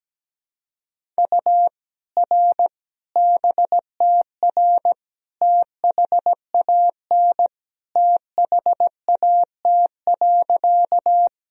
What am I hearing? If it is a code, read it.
Morse code, heard as a keyed tone, about 17 wpm, 700 Hz. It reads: U R BTR THAN THAT.
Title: Sentences from Top 400 Words and Abbreviations 17wpm